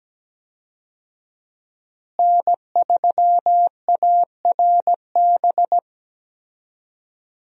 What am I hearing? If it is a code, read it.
Morse code, heard as a keyed tone, about 17 wpm, 700 Hz. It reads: N3ARB
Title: USA Call Signs - 5 Characters - Repeated 2x - 17wpm